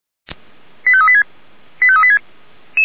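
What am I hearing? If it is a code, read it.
Five-tone selective call (ZVEI), heard as quick runs of five tones, tones of about 70 ms. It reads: 85386 85386
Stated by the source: ZVEI five-tone selective call received by SDR.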